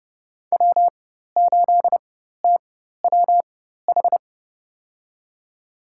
Morse code, heard as a keyed tone, about 31 wpm, spaced as for 14 wpm, 700 Hz. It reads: W8TWH
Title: USA Call Signs - 5 Characters - 4x Character Spacing - 30wpm